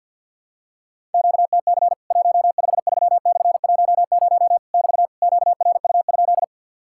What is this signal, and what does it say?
Morse code, heard as a keyed tone, about 52 wpm, 700 Hz. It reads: QTC 153/10 <BT> QRU?
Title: Worked All Europe DX Contest - QTC Exchange - 50wpm